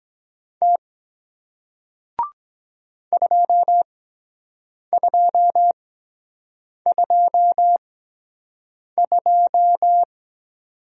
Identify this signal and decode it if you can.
Morse code, heard as a keyed tone, about 23 wpm, spaced as for 7 wpm, 700 Hz. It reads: T 2222